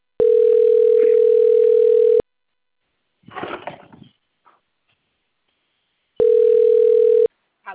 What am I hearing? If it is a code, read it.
Ringback tone (North America), 440 + 480 Hz.
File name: rg-1600-unknown- - - .2021